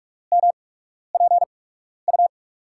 Morse code, heard as a keyed tone, about 46 wpm, spaced as for 12 wpm, 700 Hz. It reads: MPU
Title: USA Call Signs - 5 Characters - 8x Character Spacing - 45wpm